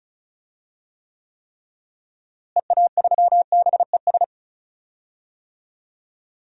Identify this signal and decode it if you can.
Morse code, heard as a keyed tone, about 34 wpm, 700 Hz. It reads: EA3BES
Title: DXCC-100 Call Signs - 6 Characters - Repeated 3x - 35wpm